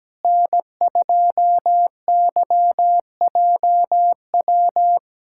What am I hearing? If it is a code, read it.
Morse code, heard as a keyed tone, about 17 wpm, 700 Hz. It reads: N2YJW
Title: USA Callsign DE Callsign 17wpm